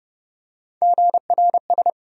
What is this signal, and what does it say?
Morse code, heard as a keyed tone, about 30 wpm, 700 Hz. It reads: GRS
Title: USA Call Sign Postfixes - 3 Letters 30wpm